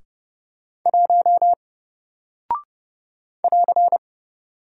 Morse code, heard as a keyed tone, about 30 wpm, 700 Hz. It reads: 1 <AR>
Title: Single Character (Letter:Number:Punctuation:Prosign) Alt 30wpm